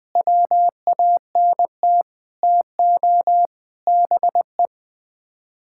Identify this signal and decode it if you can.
Morse code, heard as a keyed tone, about 20 wpm, 700 Hz. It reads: WANT TO BE
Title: Sets of 6 Words - 20wpm